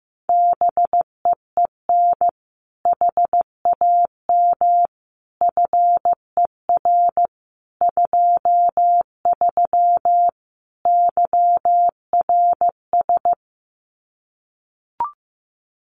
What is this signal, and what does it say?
Morse code, heard as a keyed tone, about 14 wpm, 700 Hz. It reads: BEEN HAM FER 23 YRS
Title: Common QSO Phrases Encore 15wpm